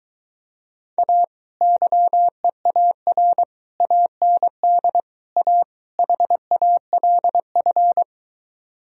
Morse code, heard as a keyed tone, about 22 wpm, 700 Hz. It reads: A YEAR AND A HALF